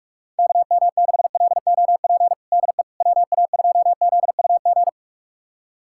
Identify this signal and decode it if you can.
Morse code, heard as a keyed tone, about 46 wpm, 700 Hz. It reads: KM6LOP DE WA2ZUG
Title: USA Callsign DE Callsign - 6 Characters - 45wpm